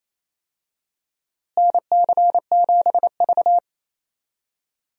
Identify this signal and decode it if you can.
Morse code, heard as a keyed tone, about 27 wpm, 700 Hz. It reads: NC7V
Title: USA Call Signs - 4 Characters - Repeated 2x - 28wpm